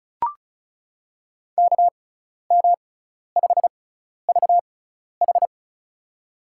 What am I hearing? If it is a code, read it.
Morse code, heard as a keyed tone, about 34 wpm, spaced as for 12 wpm, 700 Hz. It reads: KM5VH
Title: USA Call Signs - 5 Characters - 6x Character Spacing - 35wpm